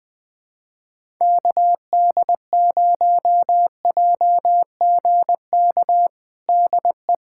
Morse code, heard as a keyed tone, about 20 wpm, 700 Hz. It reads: KD0JGK DE
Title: USA Callsign DE Callsign - 6 Characters - 20wpm